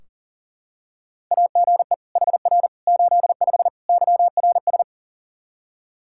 Morse code, heard as a keyed tone, about 40 wpm, 700 Hz. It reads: AGE HR 85 YRS
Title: Practice QSO - Exchange 3 - More Info - 40wpm